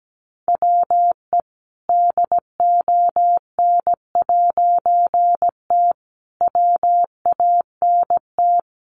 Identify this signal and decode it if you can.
Morse code, heard as a keyed tone, about 17 wpm, 700 Hz. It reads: WE DON'T WANT
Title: A Sound Of Thunder 17wpm